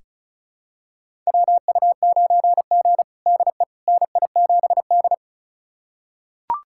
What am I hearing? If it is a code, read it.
Morse code, heard as a keyed tone, about 34 wpm, 700 Hz. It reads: WU9G DE NI7D
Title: USA Callsign DE Callsign - 4 Characters - 35wpm